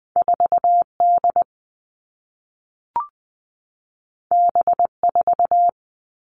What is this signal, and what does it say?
Morse code, heard as a keyed tone, about 20 wpm, 700 Hz. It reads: 4 D B 4
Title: All Cut Numbers - 2-Digit 20wpm